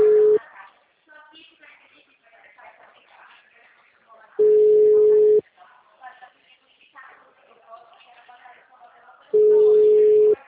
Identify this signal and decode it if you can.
Ringback tone (Europe, 425 Hz), with speech.
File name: out- -2012- - - .1860